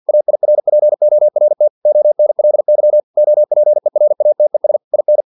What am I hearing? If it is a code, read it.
Morse code, heard as a keyed tone, about 50 wpm, 600 Hz. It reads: AIRPORT ONLY OPERATES IN
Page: Never 49wpm